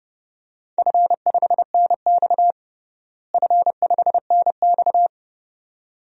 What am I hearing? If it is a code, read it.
Morse code, heard as a keyed tone, about 32 wpm, 700 Hz. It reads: F5NX F5NX
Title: DXCC-100 Call Signs - 4 Characters - Repeated 2x - 30wpm